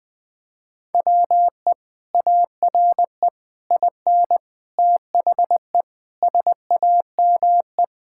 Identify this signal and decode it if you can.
Morse code, heard as a keyed tone, about 21 wpm, 700 Hz. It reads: WE ARE IN THE SAME
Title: Sentences with 3-Word Phrases - 20wpm